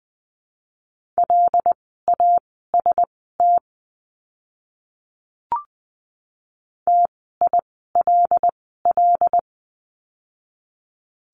Morse code, heard as a keyed tone, about 20 wpm, spaced as for 14 wpm, 700 Hz. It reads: LAST TILL
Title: 075 Lesson 13f - Letter L - Words 20wpm